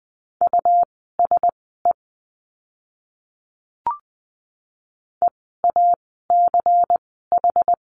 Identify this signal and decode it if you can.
Morse code, heard as a keyed tone, about 20 wpm, spaced as for 14 wpm, 700 Hz. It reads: USE EACH
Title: 192 Lesson 31f - Letter v - Words 20wpm